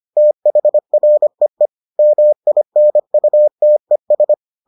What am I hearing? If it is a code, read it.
Morse code, heard as a keyed tone, about 25 wpm, 600 Hz. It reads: THREE MINUTES